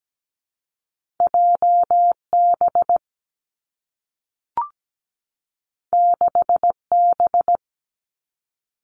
Morse code, heard as a keyed tone, about 17 wpm, 700 Hz. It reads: JB 6B
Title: Two Challenging Letter-Number Combinations 17wpm